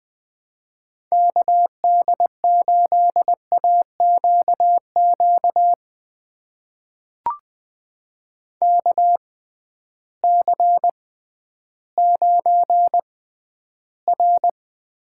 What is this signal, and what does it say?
Morse code, heard as a keyed tone, about 20 wpm, spaced as for 12 wpm, 700 Hz. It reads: KD8AQQ KC9R
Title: USA Call Signs - 6 Characters - 6x Character Spacing - 20wpm